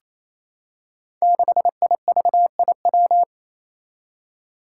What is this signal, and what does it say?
Morse code, heard as a keyed tone, about 28 wpm, 700 Hz. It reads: 6IVIW
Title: Five Challenging Letter-Number Combinations 28wpm